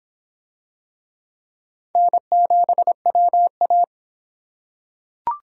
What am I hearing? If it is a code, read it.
Morse code, heard as a keyed tone, about 23 wpm, 700 Hz. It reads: N7WA